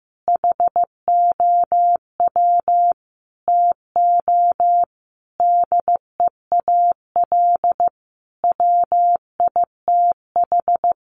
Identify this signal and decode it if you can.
Morse code, heard as a keyed tone, about 15 wpm, 700 Hz. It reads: HOW TO DEAL WITH
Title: Sentences from Top 800 Words 15wpm